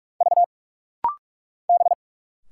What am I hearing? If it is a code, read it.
Morse code, heard as a keyed tone, about 46 wpm, 690 Hz. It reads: V B
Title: Single Letters - Rapid-Fire 45wpm